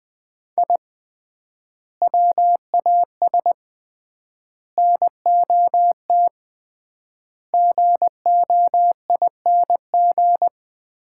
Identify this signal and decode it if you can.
Morse code, heard as a keyed tone, about 20 wpm, 700 Hz. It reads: I WAS NOT GOING
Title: 3x Word-Spacing - Sets of 5 Words - 20wpm